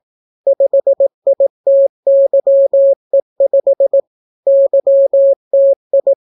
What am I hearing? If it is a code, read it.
Morse code, heard as a keyed tone, about 18 wpm, 550 Hz. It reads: 5ITYE5 YTI